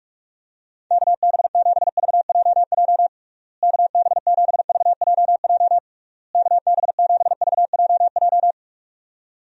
Morse code, heard as a keyed tone, about 46 wpm, 700 Hz. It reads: KB7VJJ KB7VJJ KB7VJJ
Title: USA Call Signs - 6 Characters - Repeated 3x - 45wpm